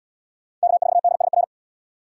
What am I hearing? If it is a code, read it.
Morse code, heard as a keyed tone, about 70 wpm, 700 Hz. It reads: K4DSR